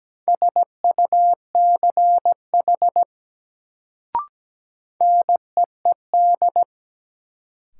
Morse code, heard as a keyed tone, about 17 wpm, 700 Hz. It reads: SUCH NEED
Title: Sentences from Top 200 Words - New Word Drill - 17wpm